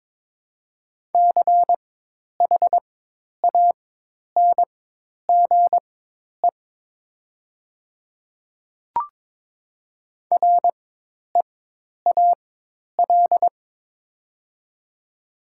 Morse code, heard as a keyed tone, about 23 wpm, spaced as for 10 wpm, 700 Hz. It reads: CHANGE REAL
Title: 4x Character Spacing - Top 500 Words - 22wpm